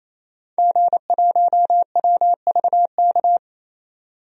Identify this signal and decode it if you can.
Morse code, heard as a keyed tone, about 28 wpm, 700 Hz. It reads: G1WVK
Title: DXCC-100 Call Signs - 5 Characters - 28wpm